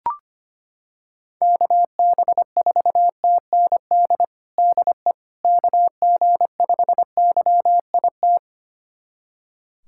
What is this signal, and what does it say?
Morse code, heard as a keyed tone, about 25 wpm, 700 Hz. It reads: KB4TND DE KG5YIT
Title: USA Callsign DE Callsign - 6 Characters - 25wpm